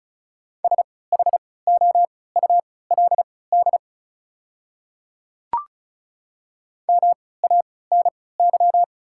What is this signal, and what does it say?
Morse code, heard as a keyed tone, about 36 wpm, spaced as for 20 wpm, 690 Hz. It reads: SHOULD MANY
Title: 3x Character Spacing - Top 300 Words - 35wpm